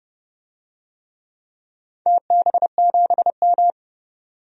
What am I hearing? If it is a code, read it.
Morse code, heard as a keyed tone, about 30 wpm, 700 Hz. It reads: TB7M
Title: DXCC-100 Call Signs - 4 Characters - 30wpm